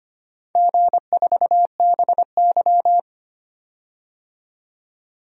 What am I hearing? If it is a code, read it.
Morse code, heard as a keyed tone, about 25 wpm, 700 Hz. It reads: G4BY
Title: DXCC-100 Call Signs - 4 Characters - 25wpm